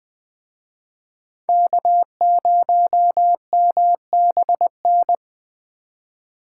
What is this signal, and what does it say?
Morse code, heard as a keyed tone, about 20 wpm, 700 Hz. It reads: K0MBN